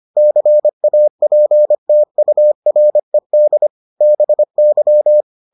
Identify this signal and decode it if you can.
Morse code, heard as a keyed tone, about 25 wpm, 600 Hz. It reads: CAPTURED BY